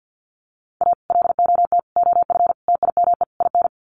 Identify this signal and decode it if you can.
Morse code, heard as a keyed tone, about 48 wpm, 700 Hz. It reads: A LOT OF TIME IN
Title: Sets of 5 Words Encore2 50wpm